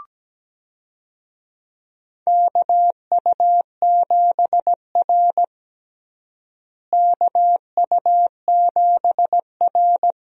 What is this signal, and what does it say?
Morse code, heard as a keyed tone, about 17 wpm, 700 Hz. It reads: KU7R KU7R
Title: USA Call Signs - 4 Characters - Repeated 2x - 17wpm